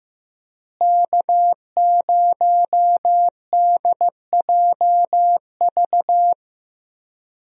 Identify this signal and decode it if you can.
Morse code, heard as a keyed tone, about 15 wpm, 700 Hz. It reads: K0DJV
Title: USA Call Signs - 5 Characters - Repeated 2x - 15wpm